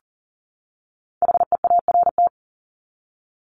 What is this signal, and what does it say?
Morse code, heard as a keyed tone, about 40 wpm, 700 Hz. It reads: HEART
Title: Top 1001-1250 Words 40wpm